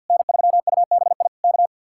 Morse code, heard as a keyed tone, about 50 wpm, 700 Hz. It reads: N3UBI K
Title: Speed-Racing Common QSO Phrases - 50wpm